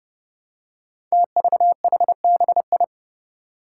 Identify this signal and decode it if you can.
Morse code, heard as a keyed tone, about 30 wpm, 700 Hz. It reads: TVHBI
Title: Five Challenging Letter-Number Combinations 30wpm